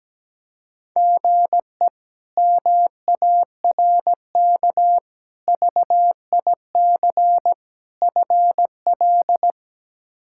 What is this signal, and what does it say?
Morse code, heard as a keyed tone, about 17 wpm, 700 Hz. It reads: GE MARK VIC FL
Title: K1USN SST QSO Practice - 17wpm